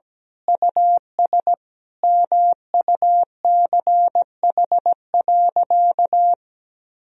Morse code, heard as a keyed tone, about 17 wpm, 700 Hz. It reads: US MUCH.
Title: Sentences from Top 1000 Words - Review Part 2 - 17wpm